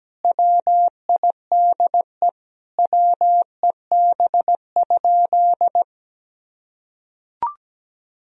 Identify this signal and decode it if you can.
Morse code, heard as a keyed tone, about 17 wpm, 700 Hz. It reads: WIDE WEB?
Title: Sentences from Top 1000 Words - Review Part 4 - 17wpm